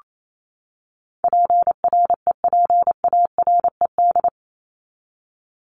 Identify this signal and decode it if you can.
Morse code, heard as a keyed tone, about 28 wpm, 700 Hz. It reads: PREPARED
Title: ED Suffix - Top 100 Words with Base Words 28wpm